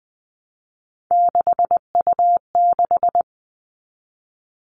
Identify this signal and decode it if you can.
Morse code, heard as a keyed tone, about 20 wpm, 700 Hz. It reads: 6U6
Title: All Cut Numbers - 3-Digit 20wpm